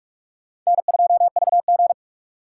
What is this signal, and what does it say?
Morse code, heard as a keyed tone, about 58 wpm, 700 Hz. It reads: N2VG